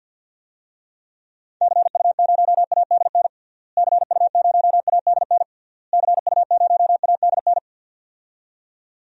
Morse code, heard as a keyed tone, about 50 wpm, 700 Hz. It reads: CU0ADN CU0ADN CU0ADN
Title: DXCC-100 Call Signs - 6 Characters - Repeated 3x - 50wpm